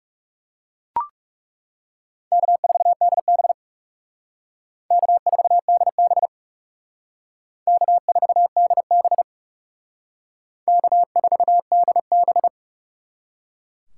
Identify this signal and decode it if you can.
Morse code, heard as a keyed tone, about 37 wpm, 700 Hz. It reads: K4DB K4DB K4DB K4DB
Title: Speed Racing - USA 1x2 Call Signs 30wpm